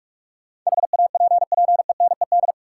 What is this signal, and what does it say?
Morse code, heard as a keyed tone, about 42 wpm, 700 Hz. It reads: HAPPENED
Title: Sentences from Top 1000 Words - Review Part 2 - New Word Drill - 45wpm